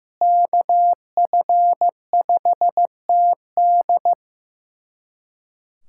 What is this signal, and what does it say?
Morse code, heard as a keyed tone, about 15 wpm, 700 Hz. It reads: KF5TD
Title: USA Callsign DE Callsign - 5 Characters - 15wpm